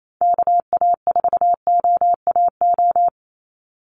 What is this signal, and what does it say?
Morse code, heard as a keyed tone, about 28 wpm, 700 Hz. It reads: KA4OAO
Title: USA Call Signs 28wpm